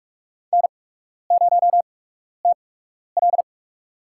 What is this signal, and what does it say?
Morse code, heard as a keyed tone, about 44 wpm, spaced as for 12 wpm, 690 Hz. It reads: N0TL
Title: USA Call Signs - 4 Characters - 8x Character Spacing - 45wpm